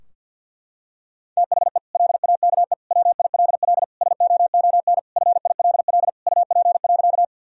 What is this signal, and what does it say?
Morse code, heard as a keyed tone, about 52 wpm, 700 Hz. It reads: THE LAKE WILL SOON FILL UP.
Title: Sentences from Top 1000 Words - Review Part 3 - 50wpm